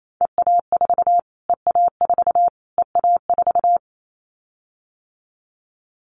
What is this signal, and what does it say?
Morse code, heard as a keyed tone, about 28 wpm, 700 Hz. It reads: EA4 EA4 EA4
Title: Callsign Prefixes for Top 25 DXCC - Repeated 3x - 28wpm